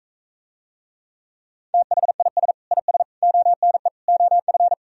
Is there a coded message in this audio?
Yes: Morse code from 1.7 to 4.7 s, heard as a keyed tone, about 45 wpm, 700 Hz. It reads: THIS IS ONE OF